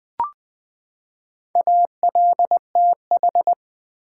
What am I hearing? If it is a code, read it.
Morse code, heard as a keyed tone, about 20 wpm, 700 Hz. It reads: ALTH
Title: Build A Word - 4 Letters At A Time - 8-Letter Words - 20wpm